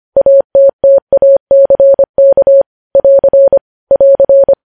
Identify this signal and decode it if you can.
Morse code, heard as a keyed tone, about 25 wpm, 550 Hz. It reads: ATTACK <AR> <AR>
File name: podcast / News headlines-1x-25x25